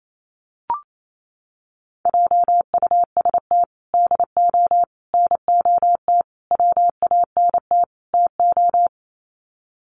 Morse code, heard as a keyed tone, about 28 wpm, 700 Hz. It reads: JUST DO NOT WANT TO
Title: Sets of 5 Words Encore 28wpm